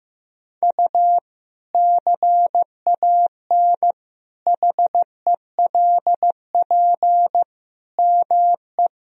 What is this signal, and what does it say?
Morse code, heard as a keyed tone, about 15 wpm, 700 Hz. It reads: U CAN HELP ME
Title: Sentences from Top 200 Words and Abbreviations 15wpm